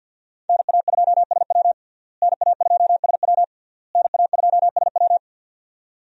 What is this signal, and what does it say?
Morse code, heard as a keyed tone, about 52 wpm, 700 Hz. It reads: NA2SW NA2SW NA2SW
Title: USA Call Signs - 5 Characters - Repeated 3x - 50wpm